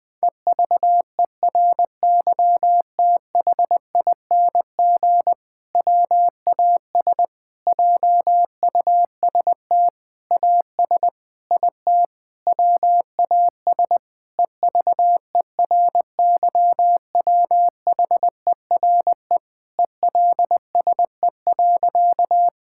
Morse code, heard as a keyed tone, about 20 wpm, 700 Hz. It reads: EVERYTHING WAS JUST AS IT WAS EVERYWHERE ELSE.
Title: Sentences from Top 1600 Words 20wpm